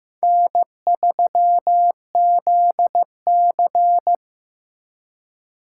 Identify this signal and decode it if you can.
Morse code, heard as a keyed tone, about 16 wpm, 700 Hz. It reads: N3ZC
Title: USA Call Signs - 4 Characters - 3x Character Spacing - 15wpm